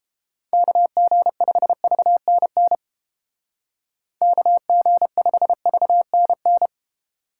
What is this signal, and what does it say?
Morse code, heard as a keyed tone, about 29 wpm, 700 Hz. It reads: KG5VNN KG5VNN